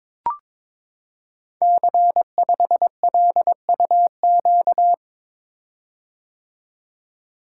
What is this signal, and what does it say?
Morse code, heard as a keyed tone, about 22 wpm, 700 Hz. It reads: C5LUQ
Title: Code Groups - 5 Letter:Numbers - 22wpm